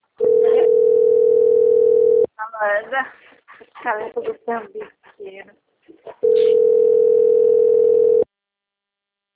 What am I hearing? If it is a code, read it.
Ringback tone (North America), 440 + 480 Hz, with speech.
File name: rg-1600-unknown- - - .50178